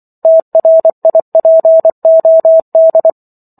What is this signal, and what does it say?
Morse code, heard as a keyed tone, about 24 wpm, 650 Hz. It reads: TRIPOD